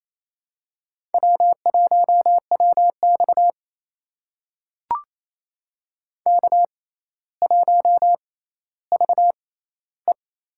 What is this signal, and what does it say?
Morse code, heard as a keyed tone, about 28 wpm, spaced as for 14 wpm, 700 Hz. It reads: W1WX K1VE
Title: USA Call Signs - 6x Character Spacing - 28wpm